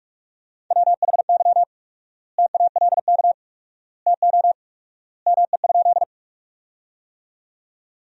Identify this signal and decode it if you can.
Morse code, heard as a keyed tone, about 47 wpm, 700 Hz. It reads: WHY TALK TO ME?
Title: 4x Word-Spacing - Sentences from Top 400 Words - 45wpm